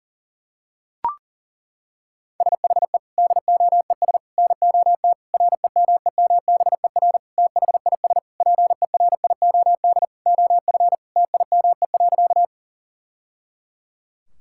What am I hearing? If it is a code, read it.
Morse code, heard as a keyed tone, about 38 wpm, 700 Hz. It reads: SHE DOES NOT REMEMBER THIS PERIOD OF TIME.
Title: Sentences from Top 1000 Words - Review Part 5 - 40wpm